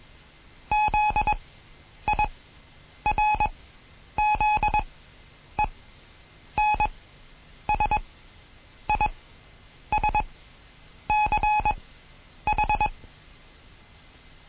Morse code, heard as a keyed tone, about 21 wpm, spaced as for 9 wpm, 840 Hz. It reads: ZIRZENSISCH
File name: Morsen / Kurs A 20